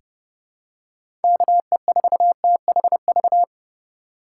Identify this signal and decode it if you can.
Morse code, heard as a keyed tone, about 32 wpm, 700 Hz. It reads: KE4THV
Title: USA Call Signs - 6 Characters - 4x Character Spacing - 30wpm